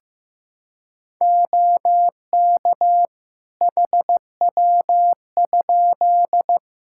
Morse code, heard as a keyed tone, about 15 wpm, 700 Hz. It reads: OK HW?
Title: K1USN SST QSO Practice 15wpm